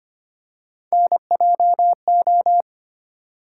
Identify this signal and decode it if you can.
Morse code, heard as a keyed tone, about 25 wpm, 700 Hz. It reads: NJO
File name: Three Letter-Number Combinations - 25wpm